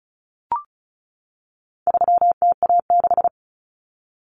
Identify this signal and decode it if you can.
Morse code, heard as a keyed tone, about 34 wpm, 700 Hz. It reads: 3TA6
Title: Common Cut Numbers - 4-Digit 35wpm